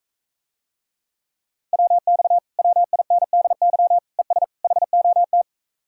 Morse code, heard as a keyed tone, about 44 wpm, 700 Hz. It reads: WX WINDY ES HOT